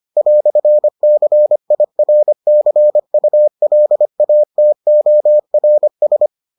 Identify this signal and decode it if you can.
Morse code, heard as a keyed tone, about 25 wpm, 600 Hz. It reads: "CIRCULATORS